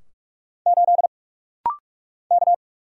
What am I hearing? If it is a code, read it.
Morse code, heard as a keyed tone, about 46 wpm, 700 Hz. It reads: 8 K
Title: Single Letter-Number - Rapid-Fire 45wpm